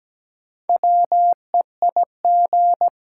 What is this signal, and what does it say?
Morse code, heard as a keyed tone, about 17 wpm, 700 Hz. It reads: WEIG
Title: Six Letter Words - 17wpm